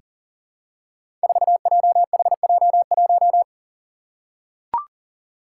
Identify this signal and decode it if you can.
Morse code, heard as a keyed tone, about 40 wpm, 700 Hz. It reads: 4JHJ1
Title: Five Challenging Letter-Number Combinations 40wpm